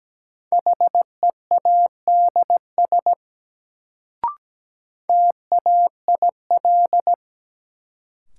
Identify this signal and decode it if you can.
Morse code, heard as a keyed tone, about 17 wpm, 700 Hz. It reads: HEADS TAIL
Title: Sentences from Top 1000 Words - Review Part 3 - New Word Drill - 17wpm